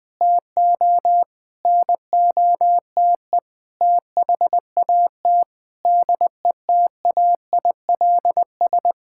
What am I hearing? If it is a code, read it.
Morse code, heard as a keyed tone, about 20 wpm, 700 Hz. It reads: TO NOTE THAT DETAILS
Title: Sentences with 5-Word Phrases - 20wpm